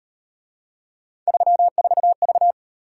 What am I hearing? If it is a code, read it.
Morse code, heard as a keyed tone, about 37 wpm, 700 Hz. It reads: 34V